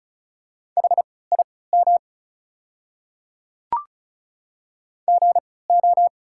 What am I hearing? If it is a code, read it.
Morse code, heard as a keyed tone, about 34 wpm, spaced as for 20 wpm, 690 Hz. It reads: HIM GO
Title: 3x Character Spacing - Top 500 Words - 35wpm